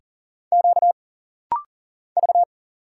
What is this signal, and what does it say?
Morse code, heard as a keyed tone, about 40 wpm, 700 Hz. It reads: Q V
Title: Single Letter-Number Alt - Rapid-Fire 40wpm